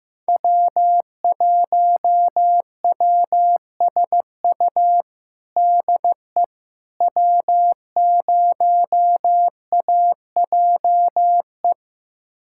Morse code, heard as a keyed tone, about 15 wpm, 700 Hz. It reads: W1WSU DE W0AJE
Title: USA Callsign DE Callsign - 5 Characters - 15wpm